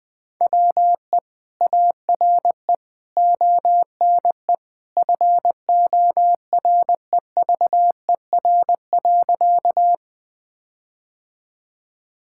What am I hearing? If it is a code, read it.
Morse code, heard as a keyed tone, about 21 wpm, 700 Hz. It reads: WE ARE ONE FOREVER.
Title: Sentences from Top 1600 Words 20wpm